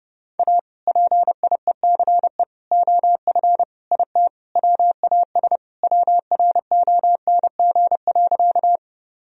Morse code, heard as a keyed tone, about 31 wpm, 700 Hz. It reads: A PIECE OF IT WAS WRONG.
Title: Sentences from Top 1000 Words - Review Part 2 - 30wpm